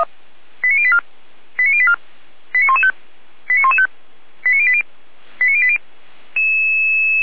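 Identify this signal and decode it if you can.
Five-tone selective call (ZVEI), heard as quick runs of five tones, tones of about 80 ms. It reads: 89084 89084 89115 89115 89080 89080
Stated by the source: ZVEI five-tone selective call received on SDR.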